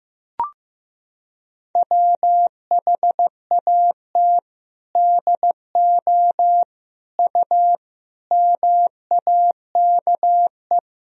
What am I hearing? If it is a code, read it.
Morse code, heard as a keyed tone, about 15 wpm, 700 Hz. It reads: WHAT DO U MAKE
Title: Sentences from Top 100 Words and Abbreviations 15wpm